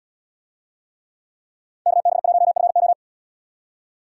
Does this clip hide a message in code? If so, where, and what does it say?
Yes: Morse code from 1.9 to 2.9 s, heard as a keyed tone, about 70 wpm, 700 Hz. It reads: KB1FO